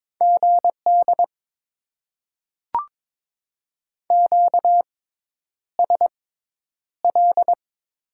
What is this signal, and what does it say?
Morse code, heard as a keyed tone, about 22 wpm, spaced as for 10 wpm, 700 Hz. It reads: GD QSL
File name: Top 90 QSO Elements - 6x Character Spacing - 22wpm